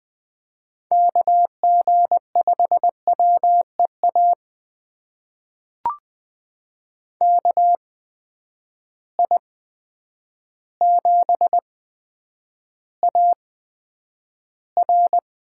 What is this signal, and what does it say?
Morse code, heard as a keyed tone, about 20 wpm, spaced as for 9 wpm, 700 Hz. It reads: KG5WEA KI7AR
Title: USA Call Signs - 6 Characters - 8x Character Spacing - 20wpm